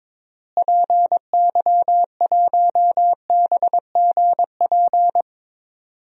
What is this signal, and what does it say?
Morse code, heard as a keyed tone, about 22 wpm, 700 Hz. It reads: PY1BGP
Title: DXCC-100 Call Signs - 6 Characters - Repeated 3x - 22wpm